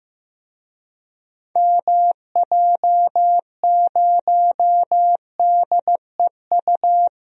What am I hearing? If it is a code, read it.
Morse code, heard as a keyed tone, about 15 wpm, 700 Hz. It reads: MJ0DEU
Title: DXCC-100 Call Signs - 6 Characters - Repeated 2x - 15wpm